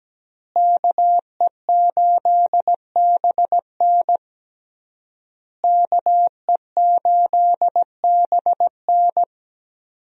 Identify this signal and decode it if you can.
Morse code, heard as a keyed tone, about 17 wpm, 700 Hz. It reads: KE8BN KE8BN
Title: USA Call Signs - 5 Characters - Repeated 3x - 17wpm